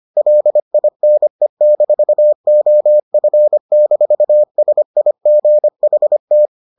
Morse code, heard as a keyed tone, about 25 wpm, 600 Hz. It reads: LINE-OF-SIGHT